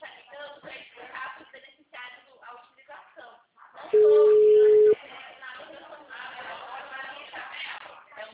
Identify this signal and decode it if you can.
Ringback tone (Europe, 425 Hz), with speech.
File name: out- -2010- - - .39151